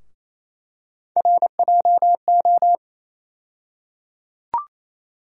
Morse code, heard as a keyed tone, about 27 wpm, 700 Hz. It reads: RJO
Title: USA Call Sign Postfixes - 3 Letters 28wpm